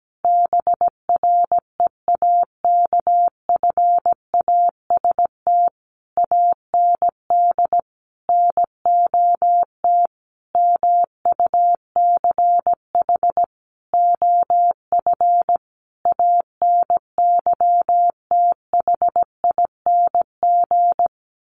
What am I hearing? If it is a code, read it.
Morse code, heard as a keyed tone, about 17 wpm, 700 Hz. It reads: BREAKFAST AND NOT MUCH OF ANYTHING
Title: Sentences from Top 1500 Words 17wpm